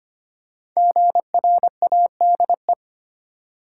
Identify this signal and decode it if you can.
Morse code, heard as a keyed tone, about 25 wpm, 700 Hz. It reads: GRADE
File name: 5 Letter Words - First 3 Letters Given - 25wpm